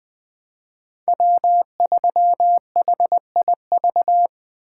Morse code, heard as a keyed tone, about 20 wpm, 700 Hz. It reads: W3HIV
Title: USA Call Signs - 5 Characters - 3x Character Spacing - 20wpm